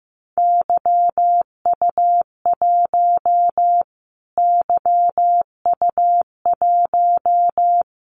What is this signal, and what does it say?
Morse code, heard as a keyed tone, about 15 wpm, 700 Hz. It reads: YU1 YU1
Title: Callsign Prefixes for Top 25 DXCC - Repeated 3x - 15wpm